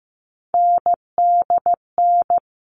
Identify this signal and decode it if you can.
Morse code, heard as a keyed tone, about 15 wpm, 700 Hz. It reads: NDN
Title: All Cut Numbers - 3-Digit 15wpm